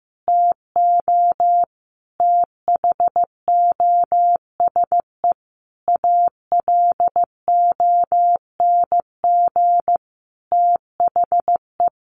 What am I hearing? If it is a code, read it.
Morse code, heard as a keyed tone, about 15 wpm, 700 Hz. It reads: TO THOSE ALONG THE
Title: Sentences from Top 1500 Words 15wpm